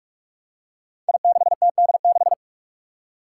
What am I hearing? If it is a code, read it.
Morse code, heard as a keyed tone, about 44 wpm, 700 Hz. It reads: I6TD6
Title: Five Challenging Letter-Number Combinations 45wpm